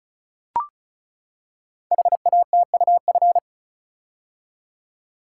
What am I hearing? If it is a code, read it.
Morse code, heard as a keyed tone, about 36 wpm, 690 Hz. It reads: HATUF
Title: Code Groups - 5 Letters - 35wpm